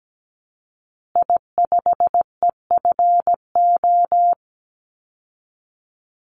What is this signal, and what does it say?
Morse code, heard as a keyed tone, about 17 wpm, 700 Hz. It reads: I5EFO
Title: Call Signs Repeated 2x - 17wpm